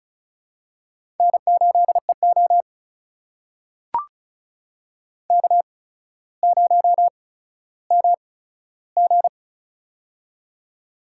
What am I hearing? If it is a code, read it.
Morse code, heard as a keyed tone, about 35 wpm, spaced as for 15 wpm, 700 Hz. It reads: N8EO K0MG
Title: USA Call Signs - 4 Characters - 8x Character Spacing - 35wpm